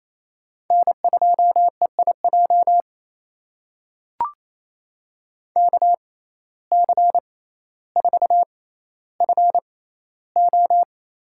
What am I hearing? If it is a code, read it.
Morse code, heard as a keyed tone, about 28 wpm, spaced as for 14 wpm, 700 Hz. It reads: N2EIJ KC4FO
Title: USA Call Signs - 5 Characters - 6x Character Spacing - 28wpm